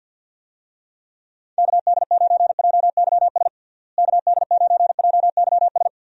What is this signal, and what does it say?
Morse code, heard as a keyed tone, about 52 wpm, 700 Hz. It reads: KD9JYS KD9JYS
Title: USA Call Signs - 6 Characters - Repeated 3x - 50wpm